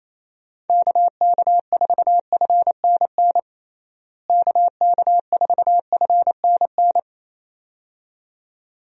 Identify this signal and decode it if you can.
Morse code, heard as a keyed tone, about 28 wpm, 700 Hz. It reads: KK4FNN KK4FNN
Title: USA Call Signs - 6 Characters - Repeated 2x - 28wpm